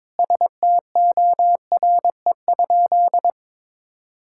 Morse code, heard as a keyed tone, about 22 wpm, 700 Hz. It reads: STORE?
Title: Sentences from Top 1000 Words - Review Part 5 - 22wpm